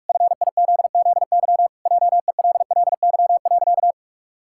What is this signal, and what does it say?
Morse code, heard as a keyed tone, about 46 wpm, 700 Hz. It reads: FIZZY JELLY.
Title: Pangram Challenge v2 - 45wpm